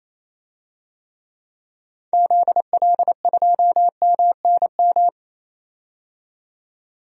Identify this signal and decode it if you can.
Morse code, heard as a keyed tone, about 28 wpm, 700 Hz. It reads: ZL2MNM
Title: DXCC-100 Call Signs - 6 Characters - 28wpm